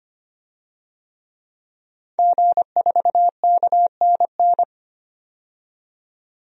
Morse code, heard as a keyed tone, about 25 wpm, 690 Hz. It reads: G4KNN
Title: DXCC-100 Call Signs - 5 Characters - 25wpm